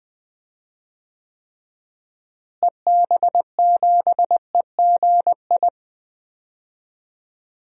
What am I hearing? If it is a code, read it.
Morse code, heard as a keyed tone, about 19 wpm, 700 Hz. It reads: EB7EGI
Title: DXCC-100 Call Signs - 6 Characters - Repeated 3x - 20wpm